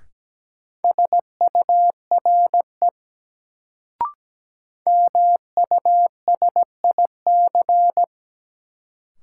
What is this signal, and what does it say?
Morse code, heard as a keyed tone, about 17 wpm, 700 Hz. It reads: SURE MUSIC
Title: Sentences from Top 300 Words - New Word Drill - 17wpm